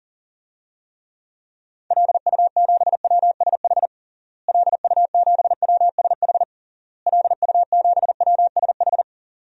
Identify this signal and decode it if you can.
Morse code, heard as a keyed tone, about 40 wpm, 700 Hz. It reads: LU7WSH LU7WSH LU7WSH
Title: DXCC-100 Call Signs - 6 Characters - Repeated 3x - 40wpm